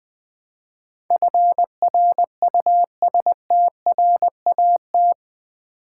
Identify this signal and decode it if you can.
Morse code, heard as a keyed tone, about 20 wpm, 700 Hz. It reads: FRUSTRAT
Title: Build A Word - 4 Letters At A Time - 11-Letter Words - 20wpm